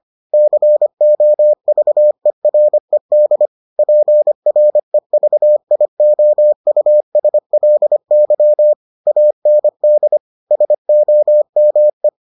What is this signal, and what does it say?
Morse code, heard as a keyed tone, about 25 wpm, 600 Hz. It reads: COVERED PREVIOUSLY AND SOME